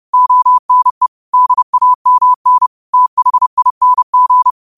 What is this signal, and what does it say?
Morse code, heard as a keyed tone, about 30 wpm, 1000 Hz. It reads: ONE DAMN THING